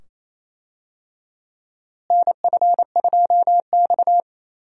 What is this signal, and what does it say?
Morse code, heard as a keyed tone, about 28 wpm, 700 Hz. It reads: NF2X
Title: USA Call Signs - 4 Characters - Repeated 2x - 28wpm